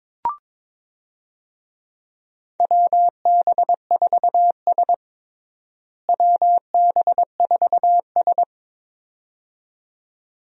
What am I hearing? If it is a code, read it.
Morse code, heard as a keyed tone, about 22 wpm, 700 Hz. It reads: WB4S WB4S
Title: USA Call Signs - 4 Characters - Repeated 2x - 22wpm